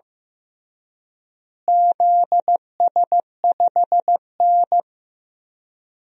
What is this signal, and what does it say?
Morse code, heard as a keyed tone, about 15 wpm, 700 Hz. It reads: ZS5N
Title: DXCC-100 Call Signs - 4 Characters - Repeated 3x - 15wpm